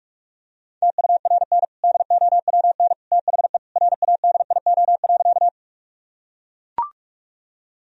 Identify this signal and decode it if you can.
Morse code, heard as a keyed tone, about 41 wpm, 700 Hz. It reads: TURN DOWN THE RADIO.
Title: Sentences from Top 1000 Words - Review Part 3 - 45wpm